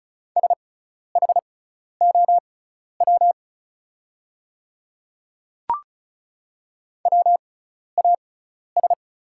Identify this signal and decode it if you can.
Morse code, heard as a keyed tone, about 36 wpm, spaced as for 12 wpm, 700 Hz. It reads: SHOW WAS
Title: 6x Character Spacing - Top 200 Words - 35wpm